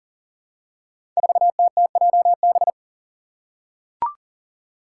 Morse code, heard as a keyed tone, about 40 wpm, 700 Hz. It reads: 4TTJB
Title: Five Challenging Letter-Number Combinations 40wpm